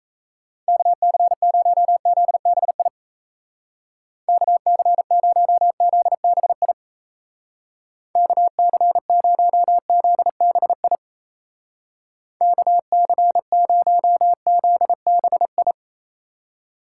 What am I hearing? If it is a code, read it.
Morse code, heard as a keyed tone, about 35 wpm, 700 Hz. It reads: KC0ZBI KC0ZBI KC0ZBI KC0ZBI